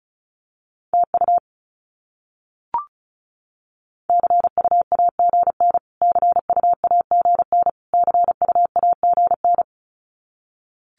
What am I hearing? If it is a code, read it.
Morse code, heard as a keyed tone, about 34 wpm, 700 Hz. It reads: TU CUAGN CUAGN CUAGN
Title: Abbreviations Repeated 3x - 35wpm